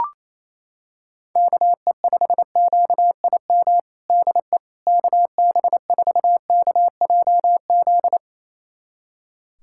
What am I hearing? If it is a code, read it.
Morse code, heard as a keyed tone, about 28 wpm, 700 Hz. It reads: KE5QIM DE KB4KJZ
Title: USA Callsign DE Callsign - 6 Characters - 28wpm